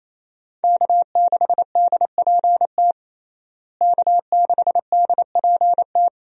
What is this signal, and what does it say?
Morse code, heard as a keyed tone, about 27 wpm, 700 Hz. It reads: K6DPT K6DPT
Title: USA Call Signs - 5 Characters - Repeated 2x - 28wpm